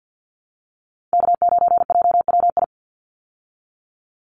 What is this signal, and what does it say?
Morse code, heard as a keyed tone, about 50 wpm, 700 Hz. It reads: K9JWI
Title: Call Signs Repeated 2x - 50wpm